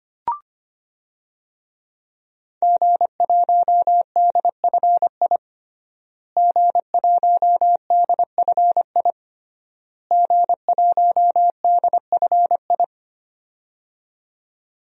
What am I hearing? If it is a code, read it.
Morse code, heard as a keyed tone, about 25 wpm, 700 Hz. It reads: G1DFI G1DFI G1DFI
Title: DXCC-100 Call Signs - 5 Characters - Repeated 3x - 25wpm